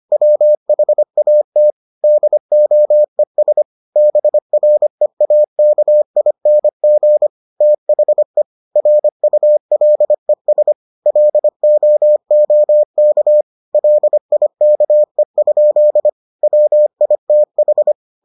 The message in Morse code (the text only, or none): WHAT DOES BREAKING THE RULES LOOK LIKE? WITH